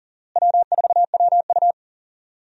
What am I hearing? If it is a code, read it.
Morse code, heard as a keyed tone, about 40 wpm, 700 Hz. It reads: W4WU
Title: USA Call Signs - 4 Characters - Repeated 2x - 40wpm